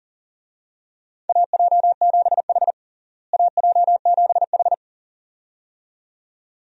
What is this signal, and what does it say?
Morse code, heard as a keyed tone, about 40 wpm, 700 Hz. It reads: AJ7H AJ7H
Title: USA Call Signs - 4 Characters - Repeated 2x - 40wpm